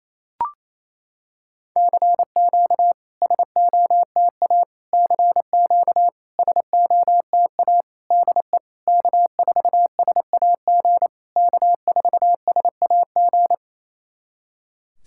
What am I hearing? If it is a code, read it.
Morse code, heard as a keyed tone, about 28 wpm, 700 Hz. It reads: CQ SOTA CQ SOTA DE K4SAG K4SAG
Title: SOTA QSO Practice 28wpm